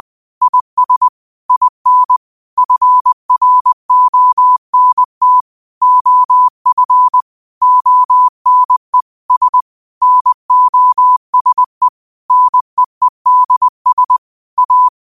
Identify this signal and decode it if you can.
Morse code, heard as a keyed tone, about 20 wpm, 1000 Hz. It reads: IS IN FRONT OF ONE S NOSE NEEDS A
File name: qotdfiles / 20wpm